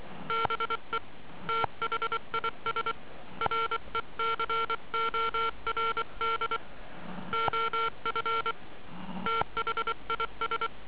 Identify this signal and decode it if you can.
Morse code, heard as a keyed tone, about 24 wpm, 1360 Hz. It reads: BE THIS RECORD OF THIS
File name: ARCHIV / zvuk2 / HB9CVQ